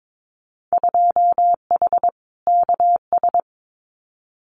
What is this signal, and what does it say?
Morse code, heard as a keyed tone, about 22 wpm, 700 Hz. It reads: 2H KS
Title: Winter Field Day Exchanges Encore 22wpm